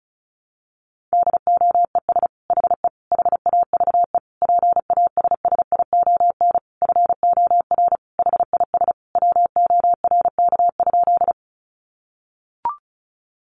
Morse code, heard as a keyed tone, about 34 wpm, 700 Hz. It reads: DOES HE HAVE PASSION FOR HIS WORK?
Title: Sentences from Top 1300 Words 35wpm